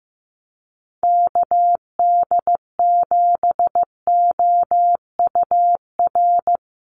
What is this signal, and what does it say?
Morse code, heard as a keyed tone, about 15 wpm, 700 Hz. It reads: KD7OUR
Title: USA Call Signs 15wpm